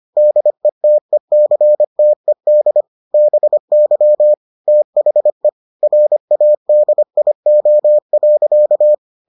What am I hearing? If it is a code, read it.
Morse code, heard as a keyed tone, about 25 wpm, 600 Hz. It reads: DETECTED BY THE RADIO.